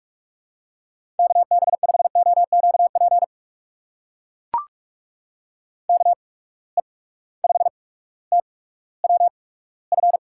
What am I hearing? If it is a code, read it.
Morse code, heard as a keyed tone, about 44 wpm, spaced as for 19 wpm, 700 Hz. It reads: KB5OQP KE5TWF
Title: USA Call Signs - 6 Characters - 8x Character Spacing - 45wpm